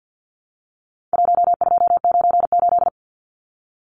Morse code, heard as a keyed tone, about 53 wpm, 700 Hz. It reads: 1298
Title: Numbers - 4-Digit 50wpm